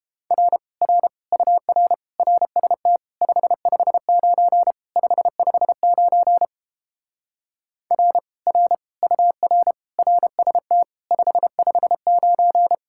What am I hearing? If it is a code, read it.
Morse code, heard as a keyed tone, about 29 wpm, 700 Hz. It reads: R R UR RST 559 559 R R UR RST 559